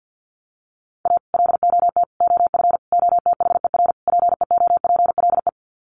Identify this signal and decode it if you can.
Morse code, heard as a keyed tone, about 48 wpm, 700 Hz. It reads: A LOT OF OTHER PEOPLE
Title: Sets of 5 Words Encore2 50wpm